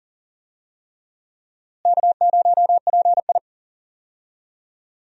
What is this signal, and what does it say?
Morse code, heard as a keyed tone, about 40 wpm, 700 Hz. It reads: K0PI
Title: USA Call Signs - 4 Characters - Repeated 3x - 40wpm